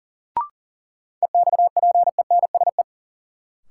Morse code, heard as a keyed tone, about 40 wpm, 700 Hz. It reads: EXPENSE
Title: Sets of 4 Words - New Word Drill - 40wpm